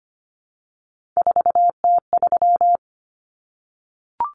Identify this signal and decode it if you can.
Morse code, heard as a keyed tone, about 25 wpm, 700 Hz. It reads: 4T3
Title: Common Cut Numbers - 3-Digit 25wpm